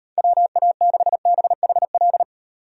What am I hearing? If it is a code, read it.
Morse code, heard as a keyed tone, about 38 wpm, 700 Hz. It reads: WA6BHL